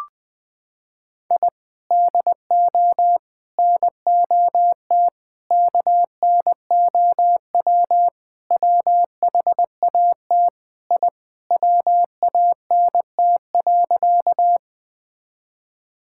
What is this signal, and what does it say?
Morse code, heard as a keyed tone, about 20 wpm, 700 Hz. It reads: I DO NOT KNOW WHAT I WANT.
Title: Sentences with 5-Word Phrases - 20wpm